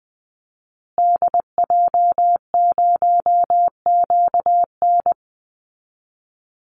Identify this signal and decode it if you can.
Morse code, heard as a keyed tone, about 20 wpm, 700 Hz. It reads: DJ0QN
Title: Call Signs Repeated 2x - 20wpm